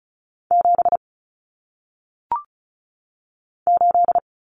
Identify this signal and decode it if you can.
Morse code, heard as a keyed tone, about 36 wpm, 700 Hz. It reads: 7 8
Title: Common Cut Numbers - 1-Digit 35wpm